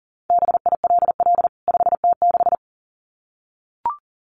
Morse code, heard as a keyed tone, about 40 wpm, 700 Hz. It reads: BILL 5T6
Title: CWT Exchange 40wpm